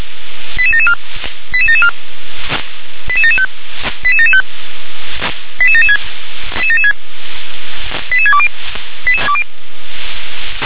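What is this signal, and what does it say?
Five-tone selective call (ZVEI), heard as quick runs of five tones, tones of about 70 ms. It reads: 88774 88774 88775 88775 88776 88620
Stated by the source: ZVEI five-tone selective call received by SDR.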